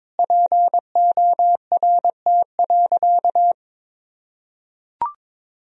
Morse code, heard as a keyed tone, about 22 wpm, 700 Hz. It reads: PORT.
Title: Sentences from Top 1000 Words - Review Part 1 - 22wpm